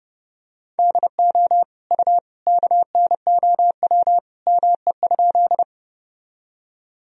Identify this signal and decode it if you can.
Morse code, heard as a keyed tone, about 30 wpm, 700 Hz. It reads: DO U KNOW ME?
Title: Sentences from Top 200 Words and Abbreviations 30wpm